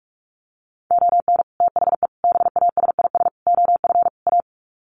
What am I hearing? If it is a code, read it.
Morse code, heard as a keyed tone, about 44 wpm, 700 Hz. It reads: ON THE BASIS OF A
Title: Sets of 5 Words Encore 45wpm